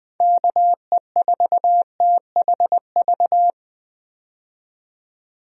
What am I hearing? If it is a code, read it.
Morse code, heard as a keyed tone, about 20 wpm, 700 Hz. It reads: KE4THV
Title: USA Call Signs - 6 Characters - 20wpm